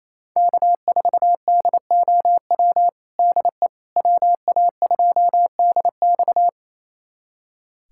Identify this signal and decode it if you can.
Morse code, heard as a keyed tone, about 28 wpm, 700 Hz. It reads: K4DOW DE WA2DX
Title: USA Callsign DE Callsign - 5 Characters - 28wpm